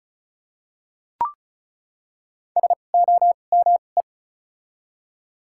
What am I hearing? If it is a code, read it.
Morse code, heard as a keyed tone, about 34 wpm, spaced as for 25 wpm, 700 Hz. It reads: SOME
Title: 2x Character Spacing - Top 300 Words - 35wpm